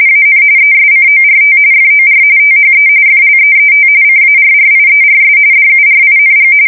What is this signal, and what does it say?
RTTY, about 45.45 baud. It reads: AN ESCORT FOR THE FLEET CARRIERS DURING
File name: fsk170 / USS-Massachusetts